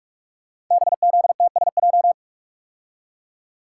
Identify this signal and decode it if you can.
Morse code, heard as a keyed tone, about 44 wpm, 700 Hz. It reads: BZTSJ
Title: Code Groups - 5 Letters - 45wpm